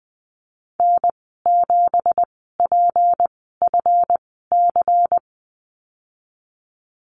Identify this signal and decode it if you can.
Morse code, heard as a keyed tone, about 20 wpm, spaced as for 15 wpm, 700 Hz. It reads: N7PFC
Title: 193 Lesson 31g - Letter v - Call Signs 20wpm